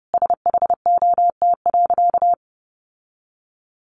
Morse code, heard as a keyed tone, about 30 wpm, 700 Hz. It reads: SHOT.
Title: Sentences from Top 1100 Words 30wpm